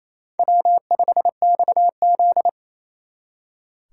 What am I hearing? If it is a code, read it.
Morse code, heard as a keyed tone, about 34 wpm, 700 Hz. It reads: W5XZ
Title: Speed Racing - USA 1x2 Call Signs 28wpm